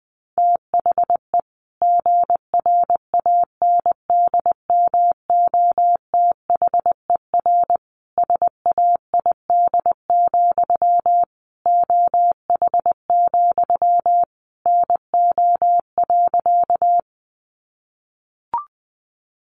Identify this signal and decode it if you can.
Morse code, heard as a keyed tone, about 19 wpm, 700 Hz. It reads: THE GRANDMOTHER SAID, OH, NO.
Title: Sentences from Top 1200 Words 20wpm